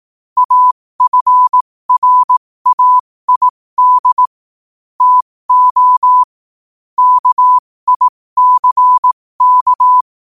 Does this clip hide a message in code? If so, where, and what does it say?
Yes: Morse code from 0.4 to 10.0 s, heard as a keyed tone, about 18 wpm, spaced as for 15 wpm, 1000 Hz. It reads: AFRAID TO KICK